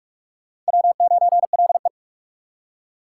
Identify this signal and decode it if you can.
Morse code, heard as a keyed tone, about 56 wpm, 700 Hz. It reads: W9LE